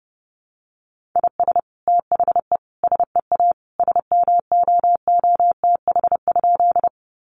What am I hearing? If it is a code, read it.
Morse code, heard as a keyed tone, about 30 wpm, 700 Hz. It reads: IS THE SEA SMOOTH?
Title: Sentences from Top 1400 Words 30wpm